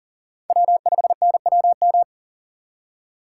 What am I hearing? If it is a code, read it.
Morse code, heard as a keyed tone, about 53 wpm, 700 Hz. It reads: W5NWM